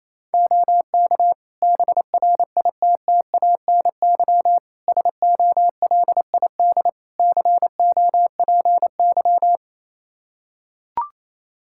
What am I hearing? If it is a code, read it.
Morse code, heard as a keyed tone, about 28 wpm, 700 Hz. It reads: OK BRITTANY SOLID COPY
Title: K1USN SST QSO Practice 28wpm